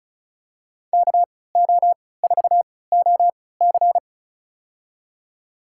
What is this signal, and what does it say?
Morse code, heard as a keyed tone, about 36 wpm, spaced as for 20 wpm, 700 Hz. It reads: KO4OC
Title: USA Call Signs - 5 Characters - 3x Character Spacing - 35wpm